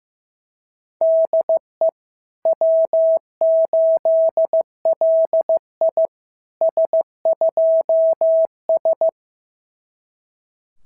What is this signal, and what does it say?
Morse code, heard as a keyed tone, about 14 wpm, 650 Hz. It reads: DE W8LI S2S
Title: SOTA QSO Practice 15wpm